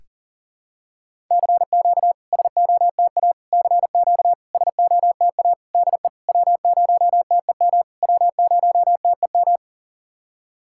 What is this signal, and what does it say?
Morse code, heard as a keyed tone, about 37 wpm, 690 Hz. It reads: CQ SOTA CQ SOTA DE W0TEM W0TEM
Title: SOTA QSO Practice 40wpm